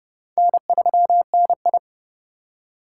Morse code, heard as a keyed tone, about 32 wpm, 700 Hz. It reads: N3NI
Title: USA Call Signs - 2x Character Spacing - 30wpm